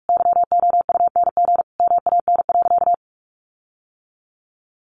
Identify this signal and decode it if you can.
Morse code, heard as a keyed tone, about 46 wpm, 700 Hz. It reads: YOUNG MAN.
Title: Sentences from Top 1300 Words 45wpm